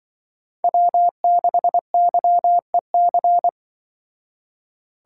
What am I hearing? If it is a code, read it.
Morse code, heard as a keyed tone, about 24 wpm, 700 Hz. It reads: W6YEC
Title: Speed Racing - USA 1x3 Call Signs 20wpm